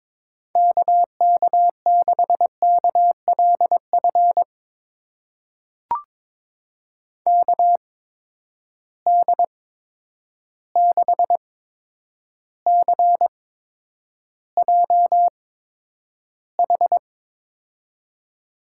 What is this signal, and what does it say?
Morse code, heard as a keyed tone, about 22 wpm, spaced as for 9 wpm, 700 Hz. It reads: KK6KLF KD6CJH
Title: USA Call Signs - 8x Character Spacing - 22wpm